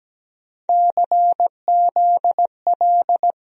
Morse code, heard as a keyed tone, about 17 wpm, 700 Hz. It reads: CZL
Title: USA Call Sign Postfixes - 3 Letters 17wpm